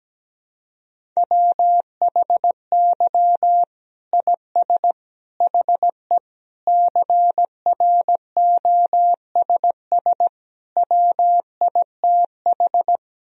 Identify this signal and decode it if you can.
Morse code, heard as a keyed tone, about 17 wpm, 700 Hz. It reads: WHY IS HE CROSS WITH
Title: Sentences from Top 1000 Words - Review Part 1 - 17wpm